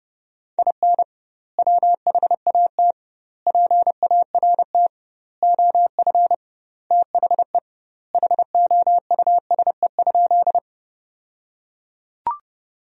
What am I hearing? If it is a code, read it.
Morse code, heard as a keyed tone, about 30 wpm, 700 Hz. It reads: IN WHAT PART OF THE HOUSE?
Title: 2x Word-Spacing - Sentences from Top 200 Words - 30wpm